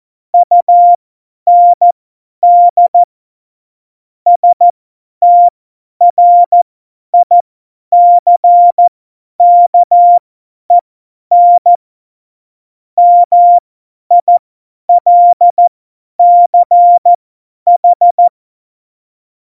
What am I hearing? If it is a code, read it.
Morse code, heard as a keyed tone, about 14 wpm, spaced as for 10 wpm, 700 Hz. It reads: UND STRICKEN MILCH
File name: Morsekurs / Witze / Witz 29. Mai 2018, 50 F70